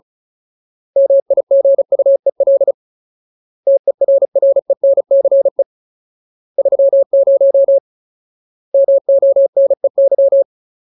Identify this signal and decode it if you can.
Morse code, heard as a keyed tone, about 34 wpm, 550 Hz. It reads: MIGUEL TERRENCE 30 MONEY
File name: Ultimate Word List / 35WPM Ultimate